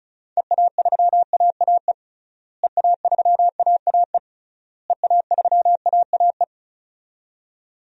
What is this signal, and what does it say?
Morse code, heard as a keyed tone, about 36 wpm, 700 Hz. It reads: EA3AAE EA3AAE EA3AAE
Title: DXCC-100 Call Signs - 6 Characters - Repeated 3x - 35wpm